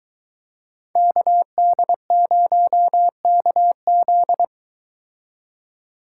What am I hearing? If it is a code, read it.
Morse code, heard as a keyed tone, about 23 wpm, 700 Hz. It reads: KD0KZ